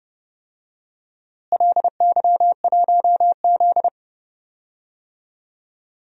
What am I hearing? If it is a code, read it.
Morse code, heard as a keyed tone, about 31 wpm, 700 Hz. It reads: LY1Z
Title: DXCC-100 Call Signs - 4 Characters - Repeated 2x - 30wpm